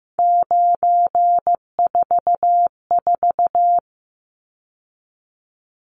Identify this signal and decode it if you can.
Morse code, heard as a keyed tone, about 15 wpm, 700 Hz. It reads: 944
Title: Numbers - 3-Digit 15wpm